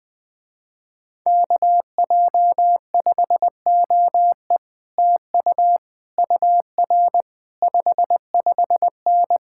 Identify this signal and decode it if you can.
Morse code, heard as a keyed tone, about 19 wpm, 700 Hz. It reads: KJ5OE TU UR 55N
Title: Speed-Racing Common QSO Phrases Encore - 20wpm